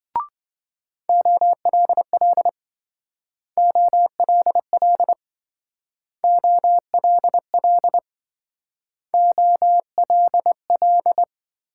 Morse code, heard as a keyed tone, about 26 wpm, 700 Hz. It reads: OLL OLL OLL OLL